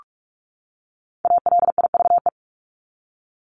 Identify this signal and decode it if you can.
Morse code, heard as a keyed tone, about 44 wpm, 700 Hz. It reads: ALIVE
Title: Top 1751-2000 Words 45wpm